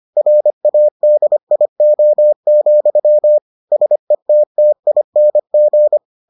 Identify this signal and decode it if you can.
Morse code, heard as a keyed tone, about 25 wpm, 600 Hz. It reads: RADIO, SETTING